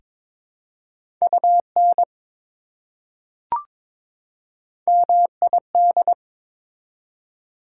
Morse code, heard as a keyed tone, about 22 wpm, 700 Hz. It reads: UN MID
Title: Prefixes and Suffixes 22wpm